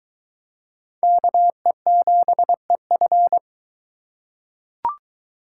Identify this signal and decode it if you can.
Morse code, heard as a keyed tone, about 20 wpm, 700 Hz. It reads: KE7EF